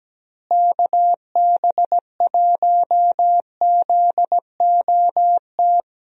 Morse code, heard as a keyed tone, about 17 wpm, 700 Hz. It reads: KB1ZOT
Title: USA Call Signs - 6 Characters - Repeated 3x - 17wpm